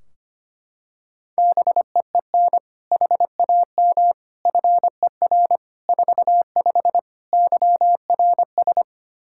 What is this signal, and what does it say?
Morse code, heard as a keyed tone, about 25 wpm, 700 Hz. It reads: BEEN HAM FER 45 YRS
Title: Practice QSO - Exchange 3 - More Info - 25wpm